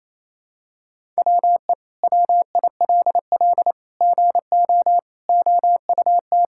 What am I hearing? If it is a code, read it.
Morse code, heard as a keyed tone, about 28 wpm, 700 Hz. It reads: WE WILL GO OUT
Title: Sentences from Top 1000 Words - Review Part 6 - 28wpm